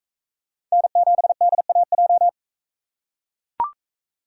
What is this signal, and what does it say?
Morse code, heard as a keyed tone, about 42 wpm, 700 Hz. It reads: N7DAJ